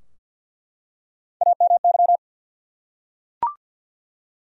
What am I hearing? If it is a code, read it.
Morse code, heard as a keyed tone, about 47 wpm, 700 Hz. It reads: AMY
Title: USA Common Names - Max 3 Letters - 50wpm